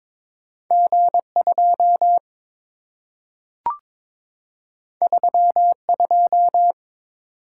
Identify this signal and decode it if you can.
Morse code, heard as a keyed tone, about 22 wpm, 700 Hz. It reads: G2 32
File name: practice / 22wpm / Two Letter-Number Combinations - 22wpm